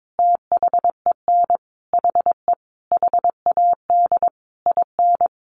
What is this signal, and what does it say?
Morse code, heard as a keyed tone, about 22 wpm, 700 Hz. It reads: THEN HE HAD IN
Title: Sentences from Top 1500 Words 22wpm